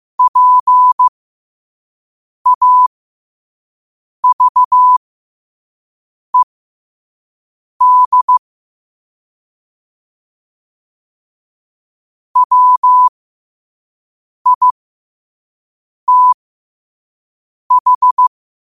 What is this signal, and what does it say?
Morse code, heard as a keyed tone, about 15 wpm, spaced as for 5 wpm, 1000 Hz. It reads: PAVED WITH
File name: qotdfiles / 5wpm